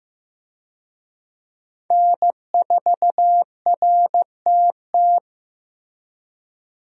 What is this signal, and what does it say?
Morse code, heard as a keyed tone, about 15 wpm, 700 Hz. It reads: N4RTT
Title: USA Call Signs - 5 Characters - Repeated 2x - 15wpm